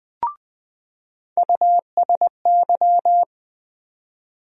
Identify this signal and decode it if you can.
Morse code, heard as a keyed tone, about 21 wpm, 700 Hz. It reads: USY
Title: USA Call Sign Postfixes - 3 Letters 20wpm